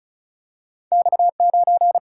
Morse code, heard as a keyed tone, about 35 wpm, 700 Hz. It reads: X 9
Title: Single-Letter Single-Number 35wpm